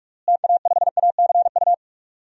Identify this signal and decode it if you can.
Morse code, heard as a keyed tone, about 44 wpm, 700 Hz. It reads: TA5ACU
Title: DXCC-100 Call Signs - 6 Characters - Repeated 2x - 45wpm